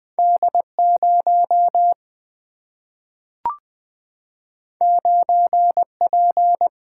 Morse code, heard as a keyed tone, about 20 wpm, 700 Hz. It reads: D0 9P
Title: Single-Letter Single-Number 20wpm